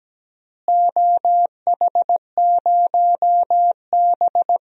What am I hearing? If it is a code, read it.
Morse code, heard as a keyed tone, about 17 wpm, 700 Hz. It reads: OH0B
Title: DXCC-100 Call Signs - 4 Characters - Repeated 3x - 17wpm